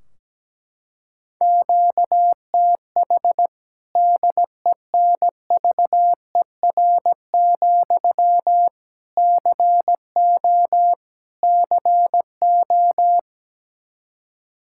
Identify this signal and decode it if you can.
Morse code, heard as a keyed tone, about 17 wpm, 700 Hz. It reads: QTH DENVER, CO CO
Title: Practice QSO - Exchange 1 - Beginning - 17wpm